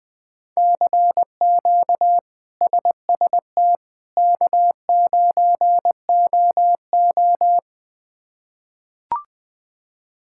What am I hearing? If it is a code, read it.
Morse code, heard as a keyed tone, about 20 wpm, 700 Hz. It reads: CQ SST K9OO
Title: K1USN SST QSO Practice - 20wpm